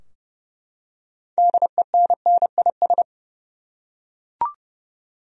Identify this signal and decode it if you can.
Morse code, heard as a keyed tone, about 30 wpm, 700 Hz. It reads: DENNIS
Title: USA Common Names - 30wpm